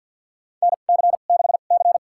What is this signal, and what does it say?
Morse code, heard as a keyed tone, about 50 wpm, spaced as for 36 wpm, 700 Hz. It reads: NC6C
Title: USA Call Signs - 4 Characters - 2x Character Spacing - 50wpm